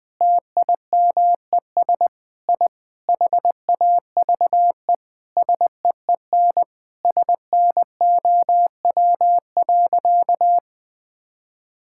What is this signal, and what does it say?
Morse code, heard as a keyed tone, about 20 wpm, 700 Hz. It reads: TIMES I HAVE SEEN SNOW.
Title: Sentences with 8-Word Phrases - 20wpm